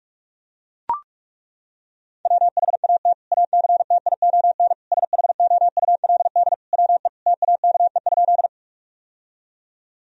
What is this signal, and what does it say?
Morse code, heard as a keyed tone, about 41 wpm, 700 Hz. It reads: WHAT ACTION SHOULD WE TAKE?
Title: Sentences from Top 1000 Words - Review Part 4 - 45wpm